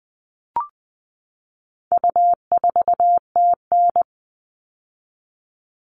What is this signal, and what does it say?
Morse code, heard as a keyed tone, about 20 wpm, 700 Hz. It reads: U4TN
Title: All Cut Numbers - 4-Digit 20wpm